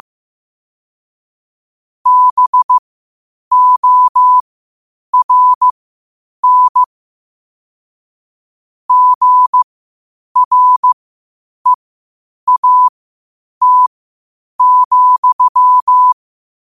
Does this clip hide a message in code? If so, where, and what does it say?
Yes: Morse code from 2.1 to 16.1 s, heard as a keyed tone, about 15 wpm, spaced as for 8 wpm, 1000 Hz. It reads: BORN GREAT,